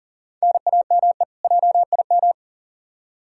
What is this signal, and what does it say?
Morse code, heard as a keyed tone, about 40 wpm, 700 Hz. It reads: NAME JIM
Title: K1USN SST QSO Practice 40wpm